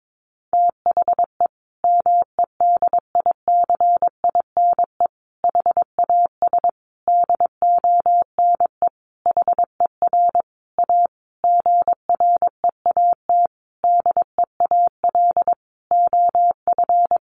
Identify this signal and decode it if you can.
Morse code, heard as a keyed tone, about 22 wpm, 700 Hz. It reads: THE MEDICINE HAS DONE HER A GREAT DEAL OF
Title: Sentences from Top 900 Words 22wpm